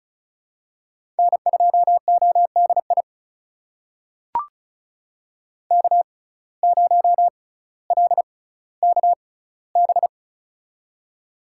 Morse code, heard as a keyed tone, about 36 wpm, spaced as for 18 wpm, 700 Hz. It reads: N2ODI K0LKB
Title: USA Call Signs - 5 Characters - 6x Character Spacing - 35wpm